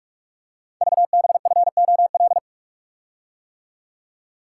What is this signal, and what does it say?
Morse code, heard as a keyed tone, about 44 wpm, 700 Hz. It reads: VBFOL
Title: Code Groups - 5 Letters - 45wpm